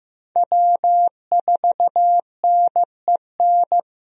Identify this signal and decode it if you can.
Morse code, heard as a keyed tone, about 15 wpm, 700 Hz. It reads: W4NEN
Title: USA Call Signs - 5 Characters - Repeated 2x - 15wpm